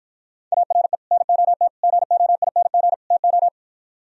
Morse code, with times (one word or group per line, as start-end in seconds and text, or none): 0.5-1.0 ARE
1.1-1.7 NOT
1.8-2.9 GOING
3.1-3.5 TO